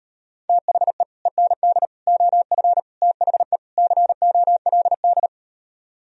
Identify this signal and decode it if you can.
Morse code, heard as a keyed tone, about 38 wpm, 700 Hz. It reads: THE END OF THE COLD